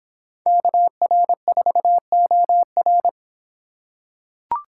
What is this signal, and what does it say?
Morse code, heard as a keyed tone, about 26 wpm, 700 Hz. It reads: KR4OR